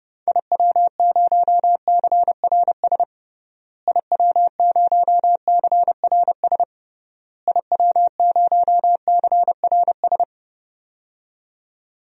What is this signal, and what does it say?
Morse code, heard as a keyed tone, about 32 wpm, 700 Hz. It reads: IW0CRS IW0CRS IW0CRS
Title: DXCC-100 Call Signs - 6 Characters - Repeated 3x - 30wpm